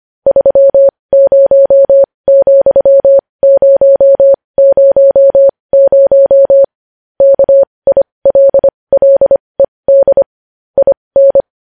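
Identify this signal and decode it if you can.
Morse code, heard as a keyed tone, about 25 wpm, spaced as for 20 wpm, 550 Hz. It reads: 30,000 KILLED IN
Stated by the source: News headlines in Morse at 20 wpm.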